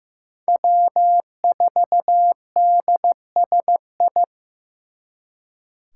Morse code, heard as a keyed tone, about 18 wpm, 700 Hz. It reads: W4DSI
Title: Speed Racing - USA Call Signs 15wpm